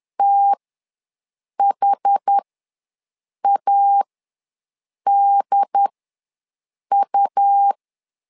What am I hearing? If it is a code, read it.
Morse code, heard as a keyed tone, about 10 wpm, spaced as for 6 wpm, 800 Hz. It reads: THADU
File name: LEZIONI / 2 UHDN / 2 UHD ws 10wpm 2058